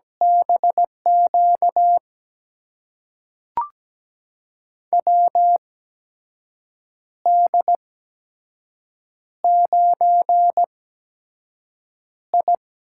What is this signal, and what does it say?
Morse code, heard as a keyed tone, about 17 wpm, spaced as for 6 wpm, 700 Hz. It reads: BQ WD9I
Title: USA Call Signs - 6 Characters - 8x Character Spacing - 17wpm